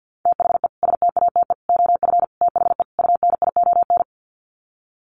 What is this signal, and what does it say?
Morse code, heard as a keyed tone, about 50 wpm, 700 Hz. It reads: THE STATE OF THE UNION
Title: Sets of 5 Words Encore 50wpm